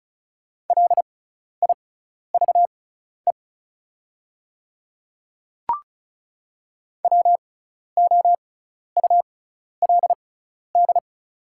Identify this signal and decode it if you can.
Morse code, heard as a keyed tone, about 36 wpm, spaced as for 12 wpm, 700 Hz. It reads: LIVE WOULD
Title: 6x Character Spacing - Top 200 Words - 35wpm